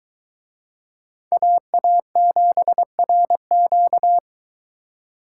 Morse code, heard as a keyed tone, about 23 wpm, 700 Hz. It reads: AA7RQ